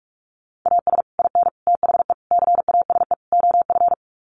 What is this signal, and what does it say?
Morse code, heard as a keyed tone, about 44 wpm, 700 Hz. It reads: AS IN THE CASE OF
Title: Sets of 5 Words 45wpm